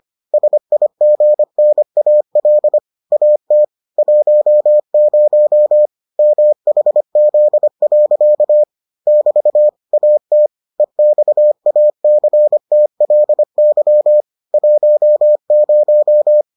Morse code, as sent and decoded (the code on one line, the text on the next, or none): ... .. --. -. .- .-.. / .- - / .---- ----- / -- .... --.. .-.-.- / -...- / .- - / . -..- .- -.-. - .-.. -.-- / .---- -----
SIGNAL AT 10 MHZ. <BT> AT EXACTLY 10